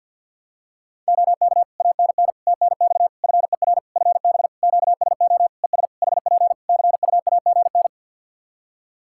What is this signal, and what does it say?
Morse code, heard as a keyed tone, about 52 wpm, 700 Hz. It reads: OK ANN TNX FER FB QSO ES HP CUAGN
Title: Practice QSO - Exchange 4 - Ending - 50wpm